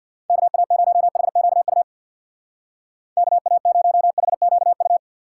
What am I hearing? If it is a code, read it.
Morse code, heard as a keyed tone, about 64 wpm, 700 Hz. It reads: KA0HQU KA0HQU